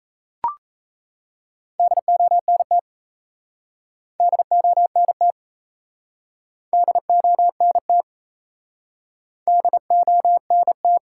Morse code, heard as a keyed tone, about 37 wpm, 700 Hz. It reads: DONT DONT DONT DONT